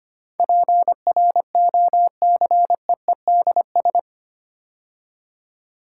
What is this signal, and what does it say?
Morse code, heard as a keyed tone, about 25 wpm, 700 Hz. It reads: PROCEEDS
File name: 8 Letter Words - First 3 Letters Given - 25wpm